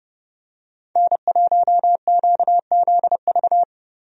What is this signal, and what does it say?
Morse code, heard as a keyed tone, about 30 wpm, 700 Hz. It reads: N1QZV
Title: USA Call Signs - 5 Characters - Repeated 2x - 30wpm